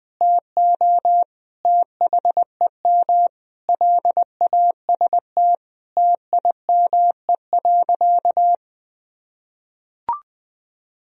Morse code, heard as a keyed tone, about 20 wpm, 700 Hz. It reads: TO THEM LAST TIME.
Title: Sentences from Top 1000 Words - Review Part 3 - 20wpm